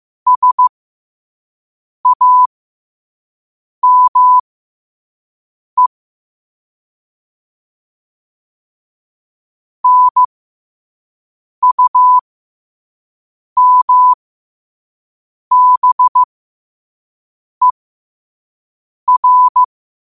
Morse code, heard as a keyed tone, about 15 wpm, spaced as for 5 wpm, 1000 Hz. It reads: SAME NUMBER
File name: qotdfiles / 5wpm